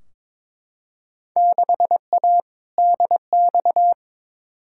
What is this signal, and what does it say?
Morse code, heard as a keyed tone, about 22 wpm, 700 Hz. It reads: 6A DX
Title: Field Day Exchanges Encore - 22wpm